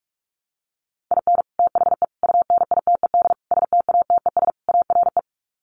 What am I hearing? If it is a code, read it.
Morse code, heard as a keyed tone, about 44 wpm, 700 Hz. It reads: IN THE UNITED STATES ARE
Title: Sets of 5 Words Encore 45wpm